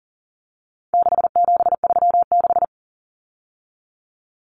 Morse code, heard as a keyed tone, about 40 wpm, 700 Hz. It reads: 6736
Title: Common Cut Numbers - 4-Digit 40wpm